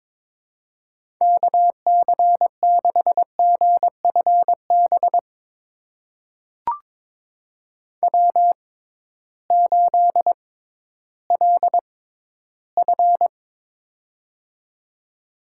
Morse code, heard as a keyed tone, about 22 wpm, spaced as for 13 wpm, 700 Hz. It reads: KC6GFB W8LF
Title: USA Call Signs - 6x Character Spacing - 22wpm